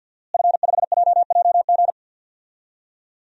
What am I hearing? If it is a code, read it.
Morse code, heard as a keyed tone, about 49 wpm, 700 Hz. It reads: F5JJG
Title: DXCC-100 Call Signs - 5 Characters - Repeated 2x - 50wpm